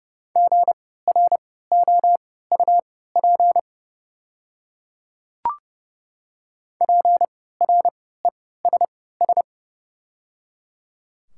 Morse code, heard as a keyed tone, about 29 wpm, spaced as for 17 wpm, 700 Hz. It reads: GROUP PRESS
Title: 3x Character Spacing - Top 300 Words - 30wpm